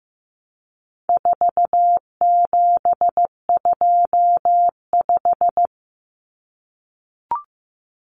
Morse code, heard as a keyed tone, about 15 wpm, 700 Hz. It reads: 4725
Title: Numbers - 4-Digit 15wpm